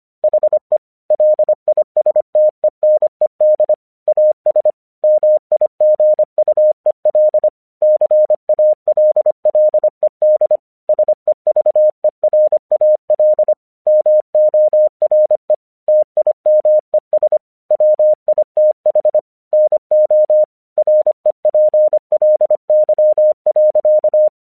Morse code, heard as a keyed tone, about 25 wpm, 610 Hz. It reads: HE LISTENED AS MIGUEL CALLED SEVERAL MORE TIMES WITH NO REPLY.
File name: 1 English 25wpm / AQC30